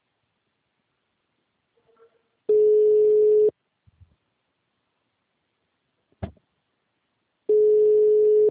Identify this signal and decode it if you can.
Ringback tone (Europe, 425 Hz).
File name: out- -2001- - - .39922